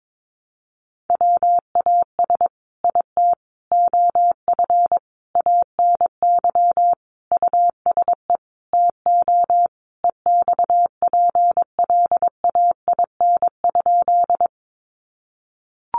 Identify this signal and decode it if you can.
Morse code, heard as a keyed tone, about 22 wpm, 700 Hz. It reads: WAS IT OF ANY USE TO EXPLAIN?
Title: Sentences from Top 1100 Words 22wpm